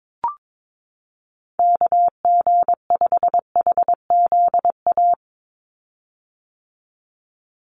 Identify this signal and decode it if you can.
Morse code, heard as a keyed tone, about 22 wpm, 700 Hz. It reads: KG5HZA
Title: USA Call Signs 22wpm